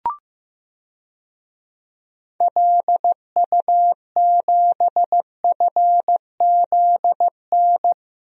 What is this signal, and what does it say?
Morse code, heard as a keyed tone, about 16 wpm, 700 Hz. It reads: LU7FZN
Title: DXCC-100 Call Signs - 6 Characters - Repeated 2x - 15wpm